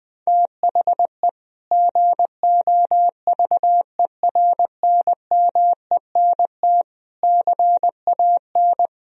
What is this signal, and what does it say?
Morse code, heard as a keyed tone, about 21 wpm, 700 Hz. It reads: THE GOVERNMENT CAN
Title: Sentences from Top 1000 Words - Review Part 3 - 20wpm